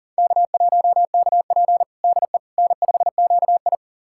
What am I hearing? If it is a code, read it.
Morse code, heard as a keyed tone, about 38 wpm, 700 Hz. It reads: K1KP DE N5QI
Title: USA Callsign DE Callsign - 4 Characters - 40wpm